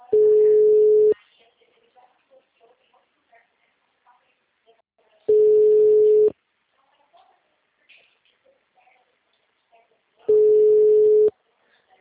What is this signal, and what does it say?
Ringback tone (Europe, 425 Hz).